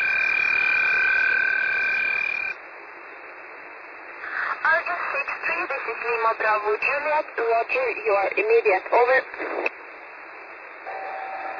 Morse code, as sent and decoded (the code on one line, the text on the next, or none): none